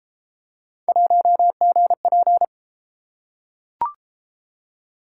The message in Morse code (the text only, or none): 1GP